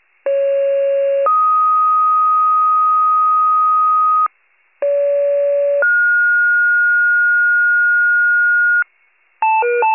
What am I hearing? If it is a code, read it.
Two-tone page, heard as a one-second tone, then a three-second tone, about 550 then 1250 Hz, about 570 then 1430 Hz, with noise. It is heard from 0.3 to 8.8 s.
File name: wcalert